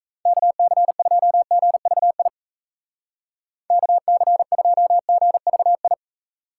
Morse code, heard as a keyed tone, about 37 wpm, 700 Hz. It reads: KC2GVI KC2GVI